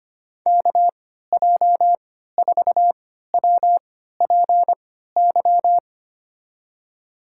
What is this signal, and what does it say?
Morse code, heard as a keyed tone, about 25 wpm, spaced as for 14 wpm, 700 Hz. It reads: KJ4WPY
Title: USA Call Signs - 6 Characters - 3x Character Spacing - 25wpm